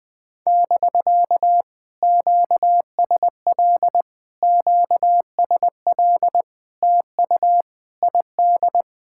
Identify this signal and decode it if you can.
Morse code, heard as a keyed tone, about 19 wpm, 700 Hz. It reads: <BK> QSL QSL TU ID
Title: POTA QSO Practice - 20wpm